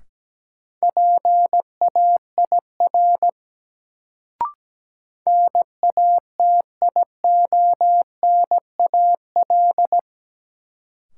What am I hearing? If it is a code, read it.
Morse code, heard as a keyed tone, about 17 wpm, 700 Hz. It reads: PAIR NATIONAL
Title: Sentences from Top 1200 Words - New Word Drill Alt - 17wpm